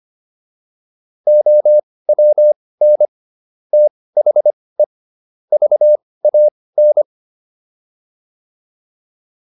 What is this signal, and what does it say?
Morse code, heard as a keyed tone, about 25 wpm, spaced as for 18 wpm, 600 Hz. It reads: OWN THE VAN